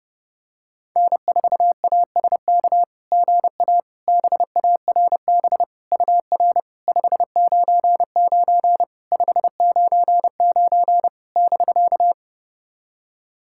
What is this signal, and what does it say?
Morse code, heard as a keyed tone, about 30 wpm, 690 Hz. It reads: N4ASK GA BARB UR 599 599 <BK>
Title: SOTA QSO Practice 30wpm